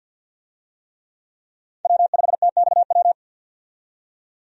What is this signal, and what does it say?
Morse code, heard as a keyed tone, about 50 wpm, 700 Hz. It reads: W5TXW
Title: USA Call Signs - 5 Characters - 50wpm